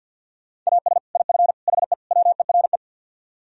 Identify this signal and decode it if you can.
Morse code, heard as a keyed tone, about 61 wpm, 700 Hz. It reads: AS IF HE WERE
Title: Speed-Racing - Sets of 4 Words 50wpm